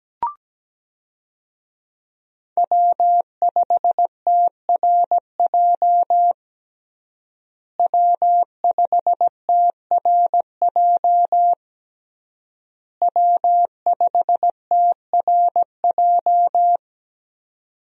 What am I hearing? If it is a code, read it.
Morse code, heard as a keyed tone, about 17 wpm, 700 Hz. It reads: W5TRJ W5TRJ W5TRJ
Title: USA Call Signs - 5 Characters - Repeated 3x - 17wpm